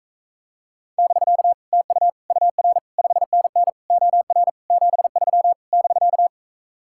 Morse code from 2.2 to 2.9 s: UR

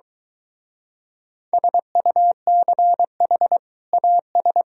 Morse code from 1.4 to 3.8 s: SUCH

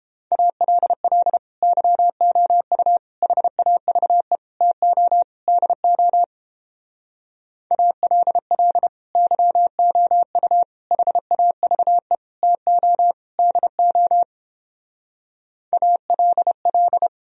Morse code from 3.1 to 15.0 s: HAVE TO DO ALL YOU HAVE TO DO

